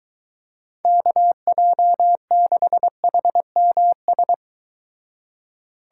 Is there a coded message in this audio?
Yes: Morse code from 0.9 to 4.3 s, heard as a keyed tone, about 24 wpm, 700 Hz. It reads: KJ6HMS